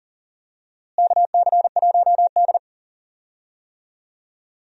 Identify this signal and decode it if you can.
Morse code, heard as a keyed tone, about 37 wpm, 700 Hz. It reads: KC1D
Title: USA Call Signs - 4 Characters - Repeated 2x - 40wpm